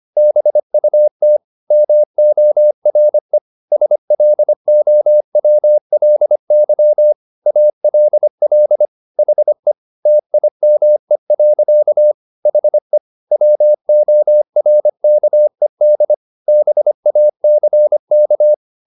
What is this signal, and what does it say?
Morse code, heard as a keyed tone, about 25 wpm, 600 Hz. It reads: BUT MORE SLOWLY ALL HE TIME. HE WORKED BACK